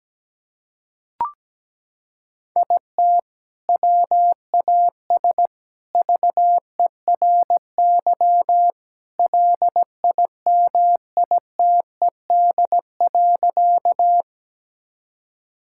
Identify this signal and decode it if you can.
Morse code, heard as a keyed tone, about 17 wpm, 700 Hz. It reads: IT WAS VERY LIMITED.
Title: Sentences from Top 1000 Words - Review Part 6 - 17wpm